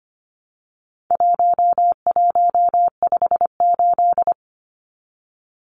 Morse code, heard as a keyed tone, about 25 wpm, 700 Hz. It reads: 1158
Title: Numbers - 4-Digit 25wpm